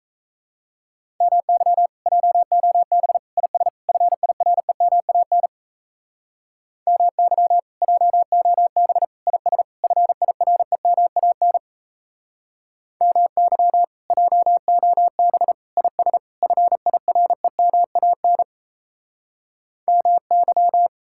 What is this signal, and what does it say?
Morse code, heard as a keyed tone, about 35 wpm, 700 Hz. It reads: MY JOB IS FIREMAN MY JOB IS FIREMAN MY JOB IS FIREMAN MY